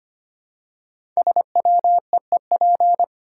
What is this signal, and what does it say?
Morse code, heard as a keyed tone, about 25 wpm, 700 Hz. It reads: SWEEP
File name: 5 Letter Words - First 3 Letters Given - 25wpm